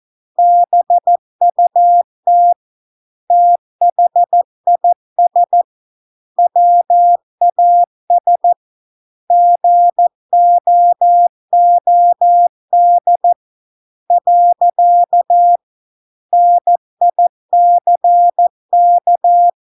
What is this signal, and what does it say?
Morse code, heard as a keyed tone, about 14 wpm, 700 Hz. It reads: BUT THIS WAS GOOD . NICK